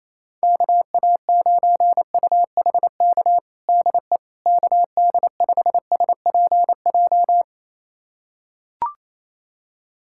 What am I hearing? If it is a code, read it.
Morse code, heard as a keyed tone, about 28 wpm, 700 Hz. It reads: KA9UHK DE KD5SPJ
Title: USA Callsign DE Callsign - 6 Characters - 28wpm